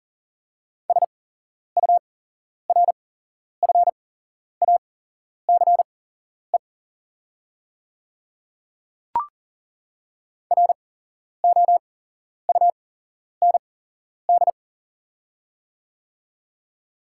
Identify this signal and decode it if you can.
Morse code, heard as a keyed tone, about 40 wpm, spaced as for 11 wpm, 700 Hz. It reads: SURFACE ROUND
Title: 8x Character Spacing - Top 500 Words - 40wpm